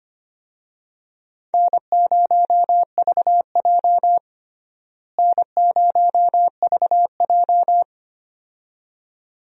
Morse code, heard as a keyed tone, about 25 wpm, 700 Hz. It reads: N0VJ N0VJ
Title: USA Call Signs - 4 Characters - Repeated 2x - 25wpm